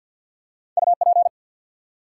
Morse code, heard as a keyed tone, about 50 wpm, 700 Hz. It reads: UP
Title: Two Letter Words - 50wpm